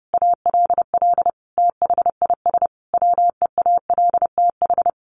Morse code, heard as a keyed tone, about 30 wpm, 700 Hz. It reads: ALL THIS WEALTH
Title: Sentences from Top 1200 Words 30wpm